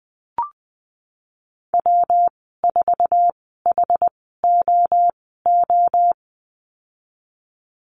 Morse code, heard as a keyed tone, about 19 wpm, spaced as for 14 wpm, 700 Hz. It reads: W4HOO
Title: 122 Lesson 20g - Number 3 - Call Signs 20wpm